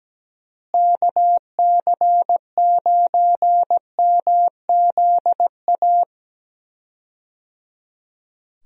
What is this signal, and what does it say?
Morse code, heard as a keyed tone, about 17 wpm, 700 Hz. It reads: KC9MZA
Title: USA Call Signs - 6 Characters - Repeated 3x - 17wpm